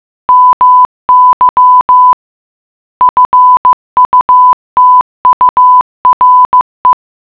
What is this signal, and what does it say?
Morse code, heard as a keyed tone, about 15 wpm, 1000 Hz. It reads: MY FUTURE